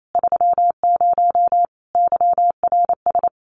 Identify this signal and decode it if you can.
Morse code, heard as a keyed tone, about 28 wpm, 700 Hz. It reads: 30 YRS
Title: Common QSO Phrases Encore 28wpm